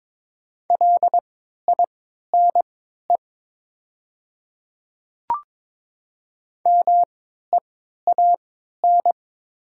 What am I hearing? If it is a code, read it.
Morse code, heard as a keyed tone, about 22 wpm, spaced as for 13 wpm, 700 Hz. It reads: LINE MEAN
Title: 3x Character Spacing - Top 200 Words - 22wpm